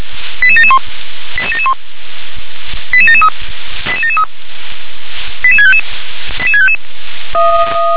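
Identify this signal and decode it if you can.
Five-tone selective call (ZVEI), heard as quick runs of five tones, tones of about 70 ms. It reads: 88771 88771 88773 88773 88655 88655
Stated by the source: ZVEI five-tone call 88727 received by SDR.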